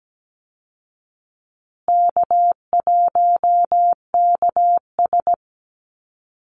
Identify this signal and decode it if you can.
Morse code, heard as a keyed tone, about 17 wpm, 700 Hz. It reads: K1KS
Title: Easy Call Signs Repeated 3x - 17wpm